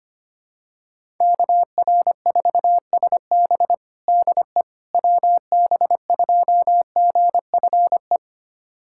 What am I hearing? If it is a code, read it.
Morse code, heard as a keyed tone, about 25 wpm, 700 Hz. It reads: KR4SB DE WB2GFE
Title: USA Callsign DE Callsign 25wpm